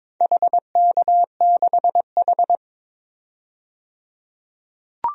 Morse code, heard as a keyed tone, about 22 wpm, 700 Hz. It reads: HK6H
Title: DXCC-100 Call Signs - 4 Characters - 22wpm